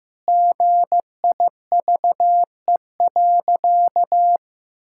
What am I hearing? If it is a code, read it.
Morse code, heard as a keyed tone, about 15 wpm, 700 Hz. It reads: GIVE.
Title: Sentences from Top 1000 Words - Review Part 6 - 15wpm